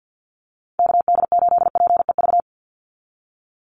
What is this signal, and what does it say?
Morse code, heard as a keyed tone, about 50 wpm, 700 Hz. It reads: KD8PEV
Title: USA Call Signs 50wpm